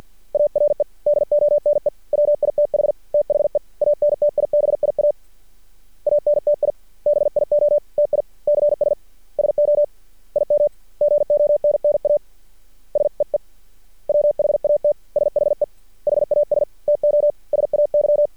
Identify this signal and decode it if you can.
Morse code, heard as a keyed tone, about 50 wpm, 590 Hz. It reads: ARE DONE WITH THE ANTIBIA ANTIBIOTICS SO IM GONNA SEE WHAT SHE HAS TO SAY
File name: archived / CRSnet